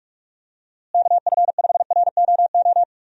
Morse code, heard as a keyed tone, about 42 wpm, 700 Hz. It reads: KF5ROO
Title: USA Call Signs - 6 Characters - Repeated 3x - 45wpm